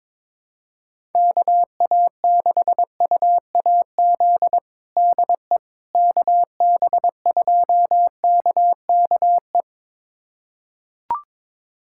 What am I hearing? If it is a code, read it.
Morse code, heard as a keyed tone, about 22 wpm, 700 Hz. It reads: KA6UAZ DE KB2KKE
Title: USA Callsign DE Callsign - 6 Characters - 22wpm